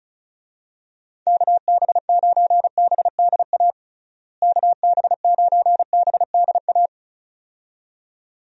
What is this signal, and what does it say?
Morse code, heard as a keyed tone, about 36 wpm, 700 Hz. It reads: KB9BDA KB9BDA
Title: USA Call Signs - 6 Characters - Repeated 2x - 35wpm